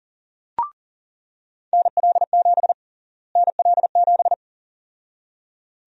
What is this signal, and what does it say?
Morse code, heard as a keyed tone, about 37 wpm, 700 Hz. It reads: NL7 NL7
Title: USA Call Sign Prefixes - Repeated 2x - 40wpm